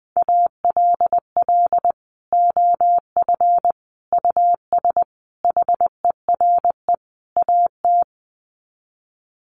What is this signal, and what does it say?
Morse code, heard as a keyed tone, about 20 wpm, 700 Hz. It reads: ALL OF US HERE AT
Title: Sets of 5 Words 20wpm